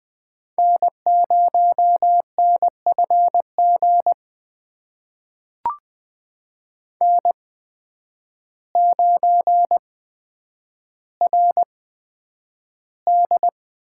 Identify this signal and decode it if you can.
Morse code, heard as a keyed tone, about 20 wpm, spaced as for 9 wpm, 700 Hz. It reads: N0NFG N9RD
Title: USA Call Signs - 5 Characters - 8x Character Spacing - 20wpm